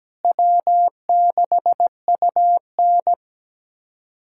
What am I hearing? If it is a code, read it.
Morse code, heard as a keyed tone, about 17 wpm, 700 Hz. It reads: W6UN
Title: USA Callsign DE Callsign - 4 Characters - 17wpm